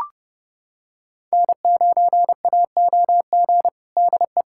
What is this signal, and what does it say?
Morse code, heard as a keyed tone, about 30 wpm, 700 Hz. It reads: N9AOG DE
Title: USA Callsign DE Callsign - 5 Characters - 30wpm